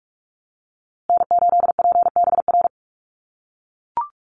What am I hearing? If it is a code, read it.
Morse code, heard as a keyed tone, about 44 wpm, 700 Hz. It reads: N8PBR
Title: USA Call Signs Repeated 2x 45wpm